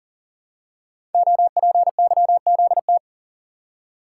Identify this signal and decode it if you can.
Morse code, heard as a keyed tone, about 39 wpm, 700 Hz. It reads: OPYZT
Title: Code Groups - 5 Letters - 40wpm